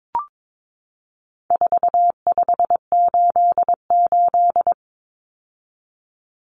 Morse code, heard as a keyed tone, about 22 wpm, 700 Hz. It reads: 4588
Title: Common Cut Numbers - 4-Digit 22wpm